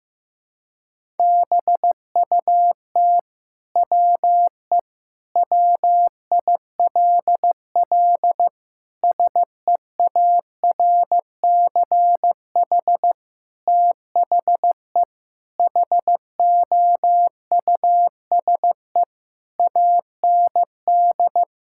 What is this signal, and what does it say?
Morse code, heard as a keyed tone, about 15 wpm, 700 Hz. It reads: BUT WE WILL SEARCH THE HOUSE AND
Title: Sentences from Top 1000 Words - Review Part 5 - 15wpm